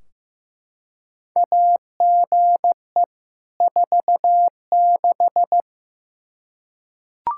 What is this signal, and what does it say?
Morse code, heard as a keyed tone, about 15 wpm, 700 Hz. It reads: AGE 46
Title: Practice QSO - Exchange 3 - More Info - 15wpm